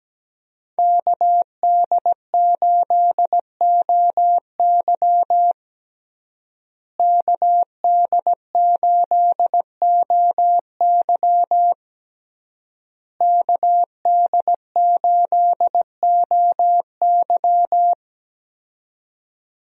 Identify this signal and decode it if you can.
Morse code, heard as a keyed tone, about 17 wpm, 700 Hz. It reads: KD8OY KD8OY KD8OY
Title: USA Call Signs - 5 Characters - Repeated 3x - 17wpm